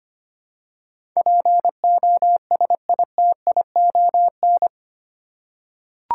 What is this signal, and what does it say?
Morse code, heard as a keyed tone, about 25 wpm, 700 Hz. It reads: POSITION
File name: 8 Letter Words - First 3 Letters Given - 25wpm